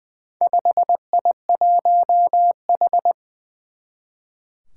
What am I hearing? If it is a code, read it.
Morse code, heard as a keyed tone, about 20 wpm, 700 Hz. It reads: 5I1H
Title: Four Challenging Letter-Number Combinations 20wpm